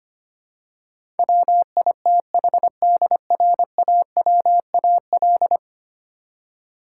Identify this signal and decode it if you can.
Morse code, heard as a keyed tone, about 25 wpm, 700 Hz. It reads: WITHDRAWAL
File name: Ten Letter Words - 25wpm